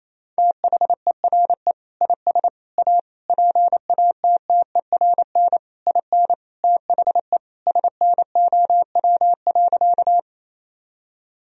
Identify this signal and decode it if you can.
Morse code, heard as a keyed tone, about 28 wpm, 700 Hz. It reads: THERE IS A PATTERN IN THE SNOW.
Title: Sentences from Top 1000 Words - Review Part 2 - 28wpm